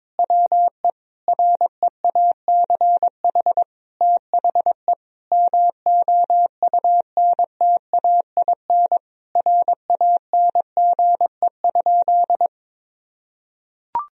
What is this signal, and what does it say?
Morse code, heard as a keyed tone, about 22 wpm, 700 Hz. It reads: WE REACH THE MOUNTAIN RANGE?
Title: Sentences from Top 1000 Words - Review Part 4 - 22wpm